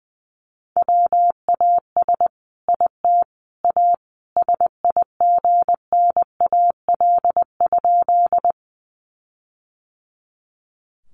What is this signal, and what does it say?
Morse code, heard as a keyed tone, about 20 wpm, 700 Hz. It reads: WAS IT A SIGNAL?
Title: Sentences from Top 1400 Words 20wpm